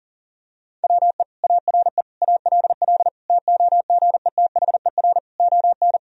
Morse code, heard as a keyed tone, about 40 wpm, 700 Hz. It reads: WE ARE ALL TOGETHER ON
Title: Sentences from Top 1000 Words - Review Part 2 - 40wpm